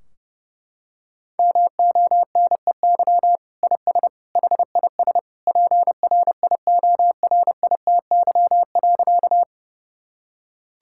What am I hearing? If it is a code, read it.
Morse code, heard as a keyed tone, about 32 wpm, 700 Hz. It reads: MONEY IS HIS PRIORITY.
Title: Sentences from Top 1600 Words 30wpm